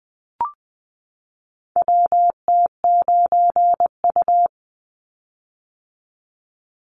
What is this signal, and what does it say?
Morse code, heard as a keyed tone, about 20 wpm, 700 Hz. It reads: WT9U
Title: Call Signs - 20wpm